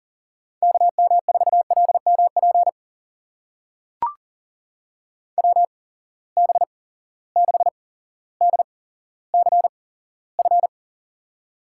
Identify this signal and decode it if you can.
Morse code, heard as a keyed tone, about 40 wpm, spaced as for 17 wpm, 700 Hz. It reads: KM4LMP WB6DCF
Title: USA Call Signs - 6 Characters - 8x Character Spacing - 40wpm